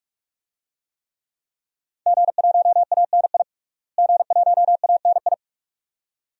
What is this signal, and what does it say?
Morse code, heard as a keyed tone, about 46 wpm, 700 Hz. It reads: G1ANI G1ANI
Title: DXCC-100 Call Signs - 5 Characters - Repeated 2x - 45wpm